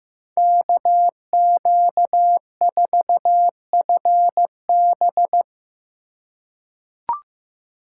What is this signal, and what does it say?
Morse code, heard as a keyed tone, about 15 wpm, 700 Hz. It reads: KQ4FB
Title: USA Call Signs - 5 Characters - 2x Character Spacing - 15wpm